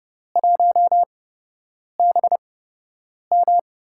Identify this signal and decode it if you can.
Morse code, heard as a keyed tone, about 31 wpm, spaced as for 8 wpm, 700 Hz. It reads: 1BM
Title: USA Call Signs - 8x Character Spacing - 30wpm